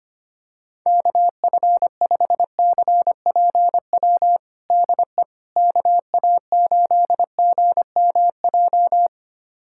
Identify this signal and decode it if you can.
Morse code, heard as a keyed tone, about 25 wpm, 700 Hz. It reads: KF5CPW DE KA8GMJ
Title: USA Callsign DE Callsign - 6 Characters - 25wpm